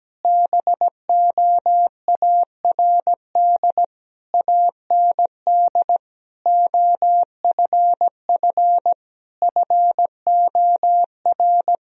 Morse code, heard as a keyed tone, about 17 wpm, 700 Hz. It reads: BOARD AND OFF FOR
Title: Sentences from Top 1000 Words - Review Part 4 - 17wpm